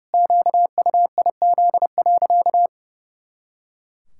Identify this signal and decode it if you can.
Morse code, heard as a keyed tone, about 32 wpm, 700 Hz. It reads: QUIZ.
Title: Pangram Challenge v2 - 30wpm